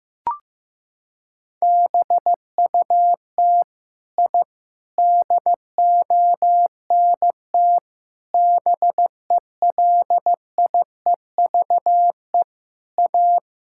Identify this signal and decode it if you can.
Morse code, heard as a keyed tone, about 15 wpm, 700 Hz. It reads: BUT I DONT BELIEVE A
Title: Sentences from Top 1000 Words - Review Part 6 - 15wpm